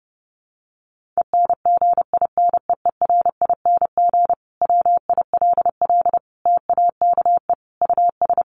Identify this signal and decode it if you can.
Morse code, heard as a keyed tone, about 30 wpm, 700 Hz. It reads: ENGINEERING WILL TAKE US
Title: Sentences from Top 1500 Words 30wpm